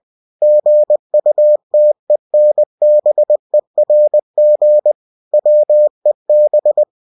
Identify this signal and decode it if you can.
Morse code, heard as a keyed tone, about 20 wpm, 600 Hz. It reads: GUTENBERG WEB
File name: cw-books / winnie-the-pooh / winnie-20wpm-page-0122